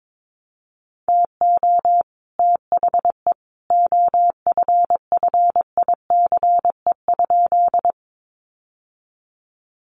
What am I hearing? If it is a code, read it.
Morse code, heard as a keyed tone, about 22 wpm, 700 Hz. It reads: TO THE OFFICE?
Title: Sentences from Top 1000 Words 22wpm